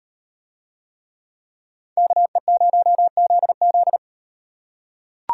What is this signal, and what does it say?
Morse code, heard as a keyed tone, about 38 wpm, 700 Hz. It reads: KE0ZZ